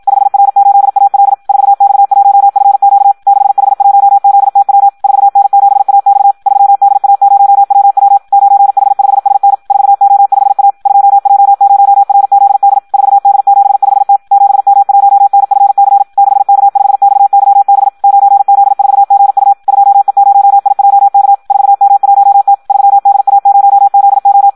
Morse code, heard as a keyed tone, about 56 wpm, 800 Hz. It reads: 5R8AL 4X1FQ 6H1ZTW 4N7AX 3DA0WW 9H5SN 4O5A 2J0RZD 4D75T 7M1NFC 6K5XYB 9Z4GU 2E0IJK 4M1T 3DA0ZO
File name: Audio practica CW / qrq-tx-54 2